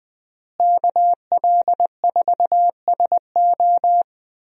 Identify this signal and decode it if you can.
Morse code, heard as a keyed tone, about 20 wpm, 700 Hz. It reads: KL4SO
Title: USA Call Signs - 5 Characters - Repeated 3x - 20wpm